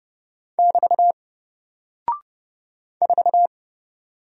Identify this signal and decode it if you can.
Morse code, heard as a keyed tone, about 30 wpm, 700 Hz. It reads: <BT> 4
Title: Single Character (Letter:Number:Punctuation:Prosign) Alt 30wpm